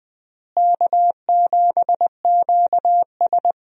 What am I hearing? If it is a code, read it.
Morse code, heard as a keyed tone, about 20 wpm, 700 Hz. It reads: K7QS
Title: USA Call Signs - 4 Characters - Repeated 3x - 20wpm